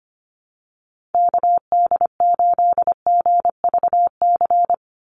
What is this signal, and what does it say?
Morse code, heard as a keyed tone, about 25 wpm, 700 Hz. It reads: KD8GVC
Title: USA Call Signs 25wpm